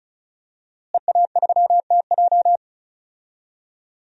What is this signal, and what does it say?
Morse code, heard as a keyed tone, about 34 wpm, 700 Hz. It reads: EA3TJ
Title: DXCC-100 Call Signs - 5 Characters - Repeated 3x - 35wpm